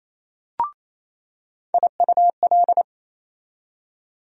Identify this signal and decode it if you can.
Morse code, heard as a keyed tone, about 28 wpm, 700 Hz. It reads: IUL
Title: USA Call Sign Postfixes - 3 Letters 28wpm